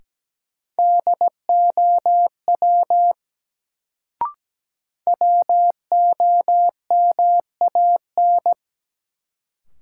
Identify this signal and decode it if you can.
Morse code, heard as a keyed tone, about 17 wpm, 700 Hz. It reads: DOW WOMAN
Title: Sets of 5 Words - New Word Drill - 17wpm